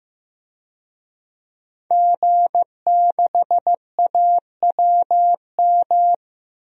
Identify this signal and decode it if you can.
Morse code, heard as a keyed tone, about 15 wpm, 700 Hz. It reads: G6AWM
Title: DXCC-100 Call Signs - 5 Characters - 15wpm